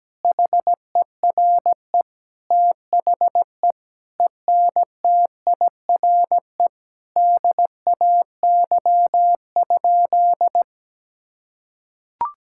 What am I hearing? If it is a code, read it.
Morse code, heard as a keyed tone, about 17 wpm, 700 Hz. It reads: HERE THE ENTIRE DAY?
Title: Sentences from Top 1000 Words - Review Part 5 - 17wpm